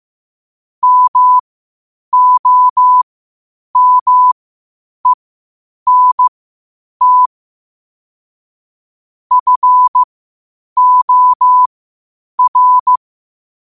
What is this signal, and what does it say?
Morse code, heard as a keyed tone, about 15 wpm, spaced as for 8 wpm, 1000 Hz. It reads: MOMENT FOR
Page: Quotes for Fri, 15 Aug 2025 in Morse Code at 8 words per minute.